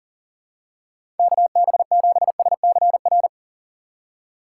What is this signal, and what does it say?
Morse code, heard as a keyed tone, about 40 wpm, 700 Hz. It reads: KB7SCR
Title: USA Call Signs - 3x Character Spacing - 40wpm